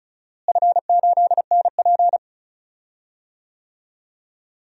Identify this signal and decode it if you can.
Morse code, heard as a keyed tone, about 36 wpm, 700 Hz. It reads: F8NP
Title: DXCC-100 Call Signs - 4 Characters - Repeated 3x - 35wpm